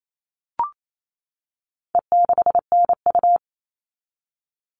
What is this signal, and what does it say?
Morse code, heard as a keyed tone, about 28 wpm, 700 Hz. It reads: E6NU
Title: All Cut Numbers - 4-Digit 28wpm